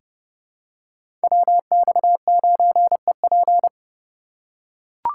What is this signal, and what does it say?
Morse code, heard as a keyed tone, about 32 wpm, 700 Hz. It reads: WX9EP
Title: USA Call Signs - 5 Characters - 4x Character Spacing - 30wpm